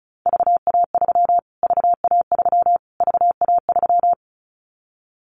Morse code, heard as a keyed tone, about 34 wpm, 700 Hz. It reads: VA3 VA3 VA3
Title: Callsign Prefixes for Top 25 DXCC - Repeated 3x - 35wpm